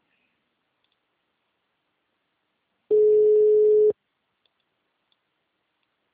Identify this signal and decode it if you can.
Ringback tone (Europe, 425 Hz).